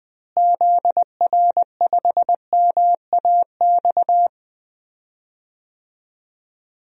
Morse code, heard as a keyed tone, about 20 wpm, 700 Hz. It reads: ZR5MAX
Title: DXCC-100 Call Signs - 6 Characters - Repeated 3x - 20wpm